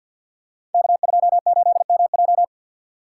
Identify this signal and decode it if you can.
Morse code, heard as a keyed tone, about 53 wpm, 700 Hz. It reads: K28MJ
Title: Code Groups - 5 Letter:Numbers - 50wpm